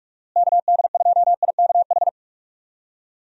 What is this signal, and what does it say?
Morse code, heard as a keyed tone, about 44 wpm, 700 Hz. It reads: KD2IKH
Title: USA Call Signs - 6 Characters - Repeated 2x - 45wpm